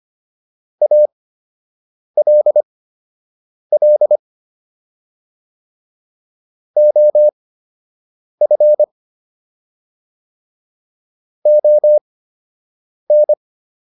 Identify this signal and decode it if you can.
Morse code, heard as a keyed tone, about 25 wpm, spaced as for 7 wpm, 600 Hz. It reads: ALL OF ON